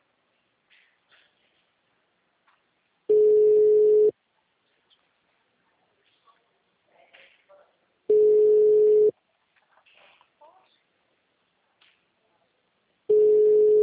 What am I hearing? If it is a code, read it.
Ringback tone (Europe, 425 Hz).
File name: out- -2007- - - .32